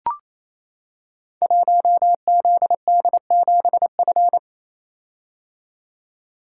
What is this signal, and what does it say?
Morse code, heard as a keyed tone, about 28 wpm, 700 Hz. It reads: 1ZD7F
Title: Code Groups - 5 Letter:Numbers - 28wpm